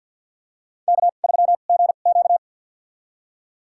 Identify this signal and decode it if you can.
Morse code, heard as a keyed tone, about 52 wpm, spaced as for 36 wpm, 700 Hz. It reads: K3GQ
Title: USA Call Signs - 4 Characters - 2x Character Spacing - 50wpm